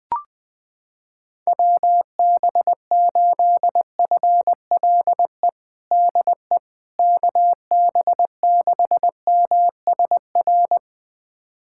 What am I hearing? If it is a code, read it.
Morse code, heard as a keyed tone, about 20 wpm, 700 Hz. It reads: WB8FLE DE KB6MSR
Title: USA Callsign DE Callsign - 6 Characters - 20wpm